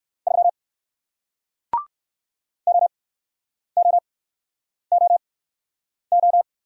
Morse code, heard as a keyed tone, about 62 wpm, spaced as for 9 wpm, 700 Hz. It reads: 3 OOOO